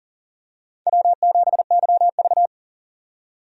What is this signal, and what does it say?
Morse code, heard as a keyed tone, about 40 wpm, 700 Hz. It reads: W7YV
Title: USA Call Signs - 4 Characters - 40wpm